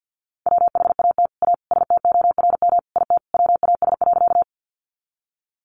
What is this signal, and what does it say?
Morse code, heard as a keyed tone, about 50 wpm, 700 Hz. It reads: WHAT A STORM IT WAS.
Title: Sentences from Top 1500 Words 50wpm